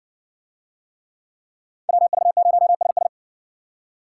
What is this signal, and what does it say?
Morse code, heard as a keyed tone, about 60 wpm, 700 Hz. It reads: WV0SS